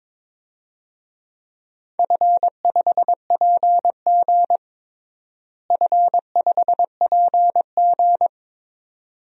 Morse code, heard as a keyed tone, about 21 wpm, 700 Hz. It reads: F5PG F5PG
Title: DXCC-100 Call Signs - 4 Characters - Repeated 2x - 22wpm